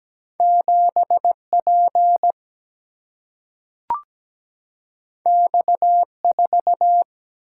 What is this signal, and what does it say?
Morse code, heard as a keyed tone, about 17 wpm, 700 Hz. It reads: 7P X4
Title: Single-Letter Single-Number 17wpm